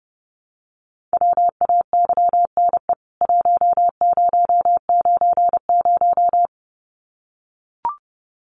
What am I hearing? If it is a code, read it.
Morse code, heard as a keyed tone, about 30 wpm, 700 Hz. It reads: WAYNE 1090
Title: CWT Exchange 30wpm